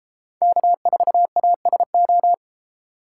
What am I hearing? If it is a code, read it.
Morse code, heard as a keyed tone, about 35 wpm, 700 Hz. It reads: K4ASO
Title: Speed Racing - USA 1x3 Call Signs 28wpm